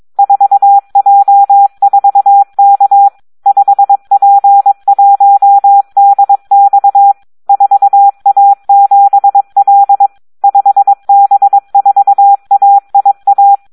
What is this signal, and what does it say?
Morse code, heard as a keyed tone, about 22 wpm, 800 Hz. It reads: 4J4K 5P1DX 4A7L 5B4AIA
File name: Audio practica CW / SD 1 22